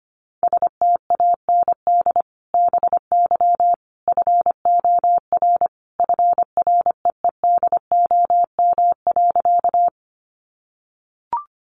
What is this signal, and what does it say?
Morse code, heard as a keyed tone, about 25 wpm, 700 Hz. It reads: STAND BY FOR FREEDOM.
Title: Sentences from Top 800 Words 25wpm